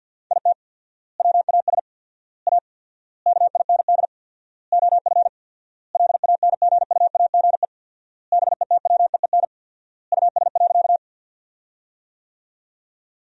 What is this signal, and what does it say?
Morse code, heard as a keyed tone, about 52 wpm, 700 Hz. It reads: IT WAS A KIND OF LANGUAGE BETWEEN US.
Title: 4x Word-Spacing - Sentences from Top 500 Words - 50wpm